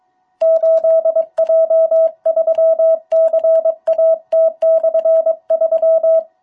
Morse code, heard as a keyed tone, about 22 wpm, 660 Hz. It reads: 8J3CAT/3
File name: jh8cbh / 2021cw